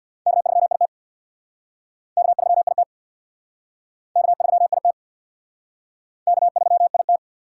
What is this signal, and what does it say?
Morse code, heard as a keyed tone, about 67 wpm, 700 Hz. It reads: K3IT K3IT K3IT K3IT